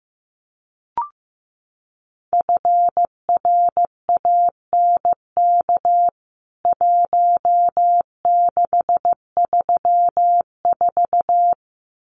Morse code, heard as a keyed tone, about 14 wpm, 700 Hz. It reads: FRANK 1634
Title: CWT Exchange 15wpm